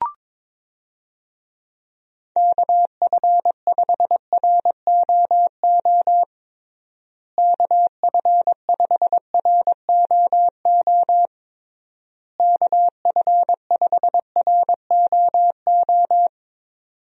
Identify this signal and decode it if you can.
Morse code, heard as a keyed tone, about 22 wpm, 700 Hz. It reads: KF5ROO KF5ROO KF5ROO
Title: USA Call Signs - 6 Characters - Repeated 3x - 22wpm